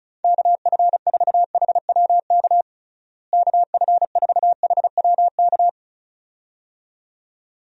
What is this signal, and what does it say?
Morse code, heard as a keyed tone, about 34 wpm, 700 Hz. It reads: KF4HWK KF4HWK
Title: USA Call Signs - 6 Characters - Repeated 2x - 35wpm